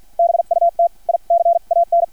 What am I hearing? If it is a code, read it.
Morse code, heard as a keyed tone, about 50 wpm, spaced as for 42 wpm, 680 Hz. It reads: BUT I CAN
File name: archived / CRSnet